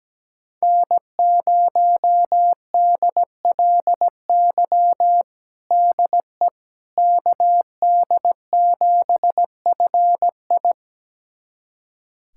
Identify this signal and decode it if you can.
Morse code, heard as a keyed tone, about 17 wpm, 700 Hz. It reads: N0DLY DE KD7FI
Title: USA Callsign DE Callsign - 5 Characters - 17wpm